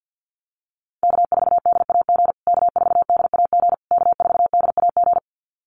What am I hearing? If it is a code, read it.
Morse code, heard as a keyed tone, about 50 wpm, 700 Hz. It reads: K4DAG K4DAG K4DAG
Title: USA Call Signs Repeated 3x 50wpm